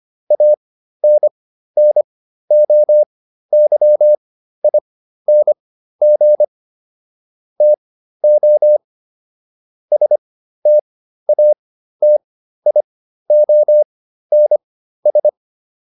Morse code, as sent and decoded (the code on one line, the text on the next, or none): .- -. -. --- -.-- .. -. --. / - --- / ... - .- - .. --- -. ...
ANNOYING TO STATIONS